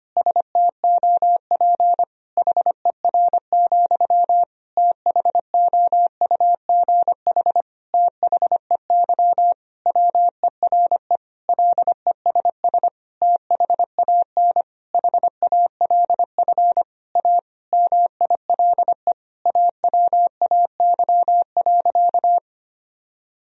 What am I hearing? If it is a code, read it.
Morse code, heard as a keyed tone, about 25 wpm, 700 Hz. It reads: STOP HER, THOUGH THEY WERE LESS THAN HALF A MILE AWAY.
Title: Sentences from Top 1000 Words - Review Part 2 - 25wpm